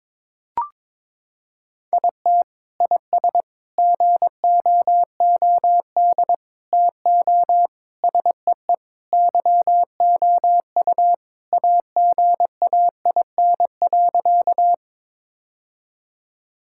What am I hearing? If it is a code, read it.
Morse code, heard as a keyed tone, about 22 wpm, 700 Hz. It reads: IT IS GOOD TO SEE YOU AGAIN.
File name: practice / 22wpm / Sentences with 4-Word Phrases - 22wpm